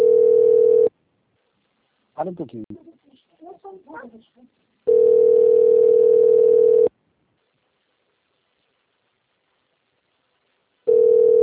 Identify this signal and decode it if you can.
Ringback tone (North America), 440 + 480 Hz, with speech.
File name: out- - - - - .16402